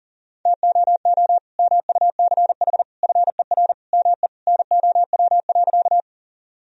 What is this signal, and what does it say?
Morse code, heard as a keyed tone, about 40 wpm, 700 Hz. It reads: TOO MUCH FER ME NOW.
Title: Sentences from Top 200 Words and Abbreviations 40wpm